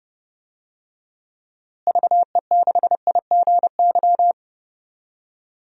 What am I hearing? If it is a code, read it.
Morse code, heard as a keyed tone, about 30 wpm, 700 Hz. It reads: VE6IGY
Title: DXCC-100 Call Signs - 6 Characters - Repeated 2x - 30wpm